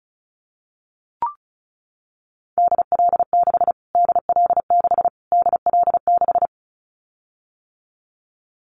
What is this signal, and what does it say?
Morse code, heard as a keyed tone, about 34 wpm, 700 Hz. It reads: DL6 DL6 DL6
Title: Callsign Prefixes for Top 25 DXCC - Repeated 3x - 35wpm